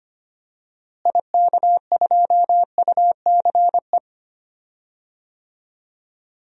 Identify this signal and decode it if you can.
Morse code, heard as a keyed tone, about 25 wpm, 700 Hz. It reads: IK2UCE
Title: DXCC-100 Call Signs - 6 Characters - Repeated 3x - 25wpm